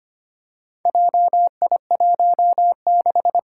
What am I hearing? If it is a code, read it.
Morse code, heard as a keyed tone, about 25 wpm, 700 Hz. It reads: JI16
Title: Four Challenging Letter-Number Combinations 25wpm